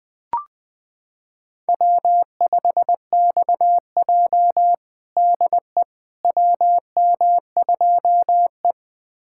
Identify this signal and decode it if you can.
Morse code, heard as a keyed tone, about 20 wpm, 700 Hz. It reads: W5XJ DE WM2E
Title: USA Callsign DE Callsign - 4 Characters - 20wpm